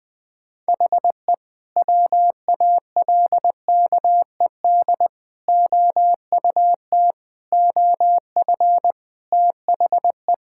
Morse code, heard as a keyed tone, about 20 wpm, 700 Hz. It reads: HE WALKED OUT OF THE
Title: Sentences with 2-Word Phrases - 20wpm